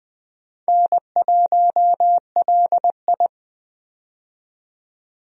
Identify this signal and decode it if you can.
Morse code, heard as a keyed tone, about 20 wpm, 700 Hz. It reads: N1LI
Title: USA Call Signs - 4 Characters - 20wpm